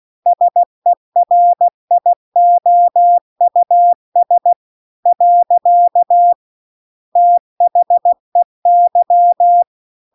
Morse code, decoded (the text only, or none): SERIOUS . THEY